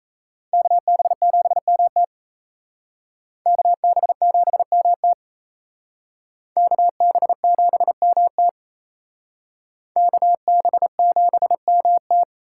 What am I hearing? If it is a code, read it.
Morse code, heard as a keyed tone, about 35 wpm, 700 Hz. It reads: KB7MT KB7MT KB7MT KB7MT